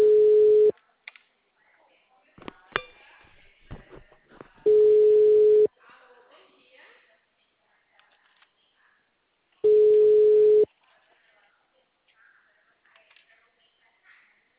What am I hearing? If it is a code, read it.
Ringback tone (Europe, 425 Hz), with speech.
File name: out- -2046- - - .9364